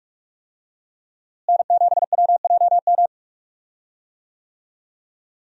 Morse code, heard as a keyed tone, about 45 wpm, 700 Hz. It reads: N7WJM
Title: USA Call Signs - 5 Characters - Repeated 2x - 45wpm